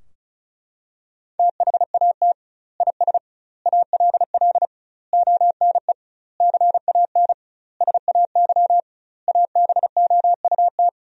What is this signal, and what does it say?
Morse code, heard as a keyed tone, about 36 wpm, 700 Hz. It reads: THAT IS ALL ONE CAN SAY ABOUT
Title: 2x Word-Spacing - Sentences from Top 200 Words - 35wpm